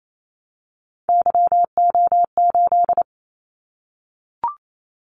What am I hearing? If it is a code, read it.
Morse code, heard as a keyed tone, about 28 wpm, 700 Hz. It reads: YO8
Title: Callsign Prefixes for Top 25 DXCC - 28wpm